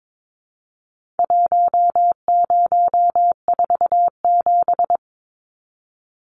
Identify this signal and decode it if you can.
Morse code, heard as a keyed tone, about 22 wpm, 700 Hz. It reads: 1047
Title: Numbers - Contest Style - 4-Digit 22wpm